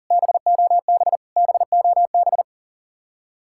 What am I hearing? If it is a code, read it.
Morse code, heard as a keyed tone, about 49 wpm, 700 Hz. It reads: BOB BOB
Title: Speed-Racing Common QSO Phrases - 40wpm